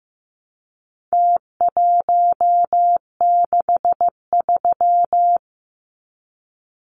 Morse code, heard as a keyed tone, about 15 wpm, 700 Hz. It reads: T163
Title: Numbers - Contest Style - 4-Digit 15wpm